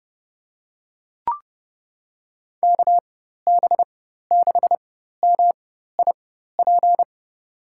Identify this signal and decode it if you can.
Morse code, heard as a keyed tone, about 32 wpm, spaced as for 14 wpm, 700 Hz. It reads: KB6MIP
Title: USA Call Signs - 6 Characters - 4x Character Spacing - 30wpm